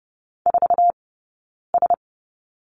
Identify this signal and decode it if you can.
Morse code, heard as a keyed tone, about 30 wpm, spaced as for 9 wpm, 700 Hz. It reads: 4S
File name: Lesson 18d - Letter m - Two Characters 30wpm